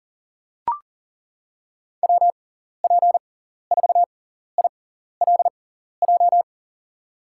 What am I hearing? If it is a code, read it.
Morse code, heard as a keyed tone, about 40 wpm, spaced as for 14 wpm, 700 Hz. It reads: WP4ILJ
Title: USA Call Signs - 6 Characters - 6x Character Spacing - 40wpm